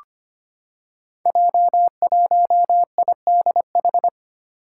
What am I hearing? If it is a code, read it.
Morse code, heard as a keyed tone, about 26 wpm, 700 Hz. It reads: J1IDH
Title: Five Challenging Letter-Number Combinations 25wpm